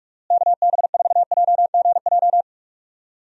Morse code, heard as a keyed tone, about 46 wpm, 700 Hz. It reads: KB4JGJ
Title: USA Call Signs - 6 Characters - Repeated 3x - 45wpm